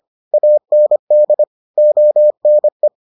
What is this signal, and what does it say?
Morse code, heard as a keyed tone, about 25 wpm, 600 Hz. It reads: AND ONE